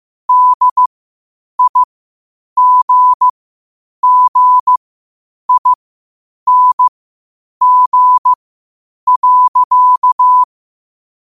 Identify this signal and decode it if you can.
Morse code, heard as a keyed tone, about 15 wpm, spaced as for 8 wpm, 1000 Hz. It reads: DIGGING.
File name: qotdfiles / 8wpm